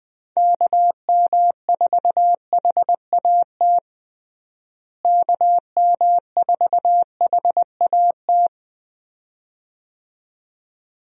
Morse code, heard as a keyed tone, about 20 wpm, 700 Hz. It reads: KM4HAT KM4HAT
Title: USA Call Signs - 6 Characters - Repeated 2x - 20wpm